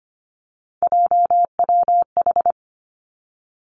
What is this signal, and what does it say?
Morse code, heard as a keyed tone, about 25 wpm, 700 Hz. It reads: JWH
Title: Three Challenging Letter-Number Combinations 25wpm